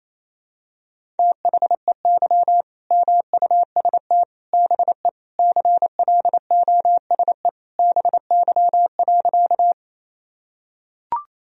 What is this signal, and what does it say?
Morse code, heard as a keyed tone, about 28 wpm, 700 Hz. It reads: THEY MUST BE CLOSE BY.
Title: Sentences from Top 1000 Words - Review Part 1 - 28wpm